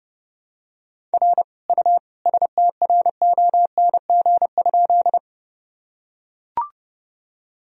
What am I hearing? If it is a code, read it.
Morse code, heard as a keyed tone, about 30 wpm, 700 Hz. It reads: R U STRONG?
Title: Sentences from Top 500 Words and Abbreviations 30wpm